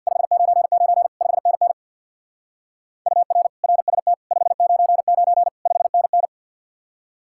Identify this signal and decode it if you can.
Morse code, heard as a keyed tone, about 62 wpm, 700 Hz. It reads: 599 5NN UR RST 599 5NN
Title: Speed-Racing Common QSO Phrases - 50wpm